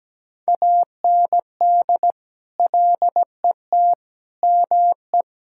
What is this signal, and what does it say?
Morse code, heard as a keyed tone, about 17 wpm, 700 Hz. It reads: AND LET ME
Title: Sentences from Top 1600 Words 17wpm